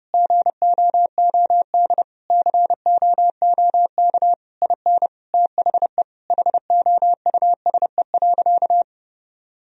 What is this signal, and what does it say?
Morse code, heard as a keyed tone, about 30 wpm, 700 Hz. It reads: GOOD COOK IN THE HOUSE.
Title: Sentences from Top 1000 Words - Review Part 4 - 30wpm